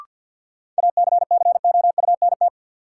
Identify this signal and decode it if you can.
Morse code, heard as a keyed tone, about 53 wpm, 700 Hz. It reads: ACCOUNT
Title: Sentences from Top 900 Words - New Word Drill - 50wpm